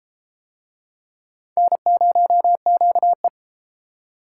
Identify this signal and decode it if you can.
Morse code, heard as a keyed tone, about 32 wpm, 700 Hz. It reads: N0QE